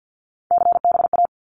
Morse code, heard as a keyed tone, about 50 wpm, 700 Hz. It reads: CBA
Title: Abbreviations 50wpm